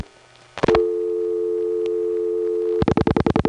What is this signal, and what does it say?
Dial tone (North America), 350 + 440 Hz.